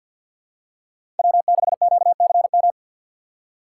Morse code, heard as a keyed tone, about 47 wpm, 700 Hz. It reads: W6QCM
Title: USA Call Signs - 5 Characters - 4x Character Spacing - 50wpm